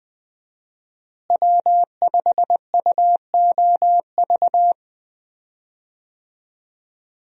Morse code, heard as a keyed tone, about 20 wpm, 700 Hz. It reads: W5UOV
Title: USA Call Signs - 5 Characters - Repeated 2x - 20wpm